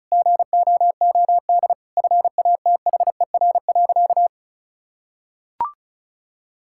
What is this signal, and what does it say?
Morse code, heard as a keyed tone, about 36 wpm, 700 Hz. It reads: GOOD FATHER.
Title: Sentences from Top 1000 Words - Review Part 5 - 35wpm